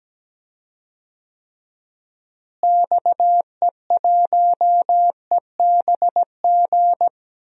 Morse code, heard as a keyed tone, about 17 wpm, 700 Hz. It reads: XE1EBG
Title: DXCC-100 Call Signs - 6 Characters - Repeated 2x - 17wpm